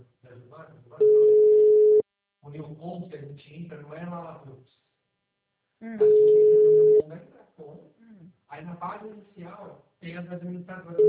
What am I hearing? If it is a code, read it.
Ringback tone (Europe, 425 Hz), with speech.